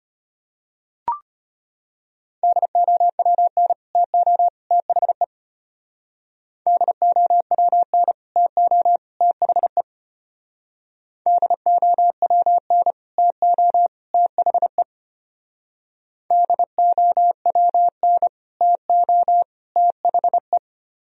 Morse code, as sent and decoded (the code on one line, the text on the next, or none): -.. --- .-- -. / - --- / - .... / -.. --- .-- -. / - --- / - .... / -.. --- .-- -. / - --- / - .... / -.. --- .-- -. / - --- / - .... .
DOWN TO THE DOWN TO THE DOWN TO THE DOWN TO THE